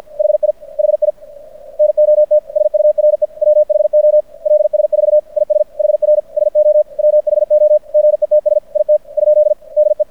Morse code, heard as a keyed tone, about 53 wpm, 600 Hz. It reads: HI HI NOT SURE WHO LS4 IS HR SO WHO GETS IT ? DE
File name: archived / CRSnet